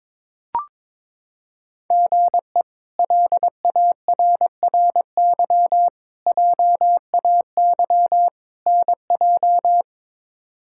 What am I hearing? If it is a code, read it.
Morse code, heard as a keyed tone, about 23 wpm, 700 Hz. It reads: GE LARRY JAY NJ
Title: K1USN SST QSO Practice - 22wpm